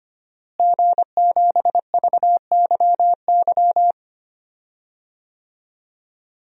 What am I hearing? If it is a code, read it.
Morse code, heard as a keyed tone, about 25 wpm, 700 Hz. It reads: G7VYY
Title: DXCC-100 Call Signs - 5 Characters - Repeated 2x - 25wpm